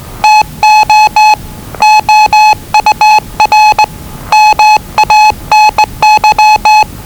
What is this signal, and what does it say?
Morse code, heard as a keyed tone, about 20 wpm, 860 Hz. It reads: TO OUR MANY
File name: WellingtonRadioZlwCloseDownOnHfAndMf / 01-Zlw 4277khz 417.5 qrt until 0925z